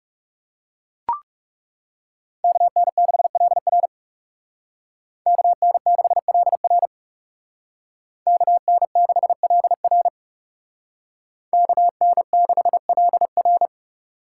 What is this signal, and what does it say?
Morse code, heard as a keyed tone, about 34 wpm, 700 Hz. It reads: KN6LR KN6LR KN6LR KN6LR
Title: Speed Racing - USA 2x2 Call Signs 30wpm